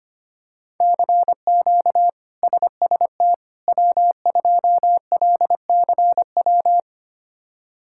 Morse code, heard as a keyed tone, about 25 wpm, 700 Hz. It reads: CQ SST W2LCW
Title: K1USN SST QSO Practice - 25wpm